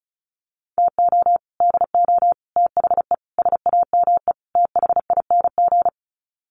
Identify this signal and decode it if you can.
Morse code, heard as a keyed tone, about 34 wpm, 700 Hz. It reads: TO DO THE SAME THING
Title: Sets of 5 Words Encore 35wpm